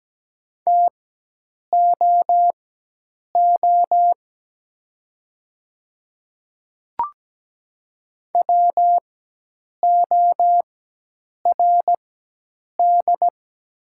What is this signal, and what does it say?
Morse code, heard as a keyed tone, about 17 wpm, spaced as for 8 wpm, 700 Hz. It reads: TOO WORD
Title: 4x Character Spacing - Top 300 Words - 17wpm